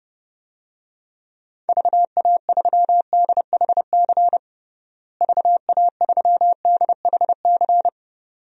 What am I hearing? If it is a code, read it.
Morse code, heard as a keyed tone, about 30 wpm, 700 Hz. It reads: VA3DHC VA3DHC
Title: DXCC-100 Call Signs - 6 Characters - Repeated 2x - 30wpm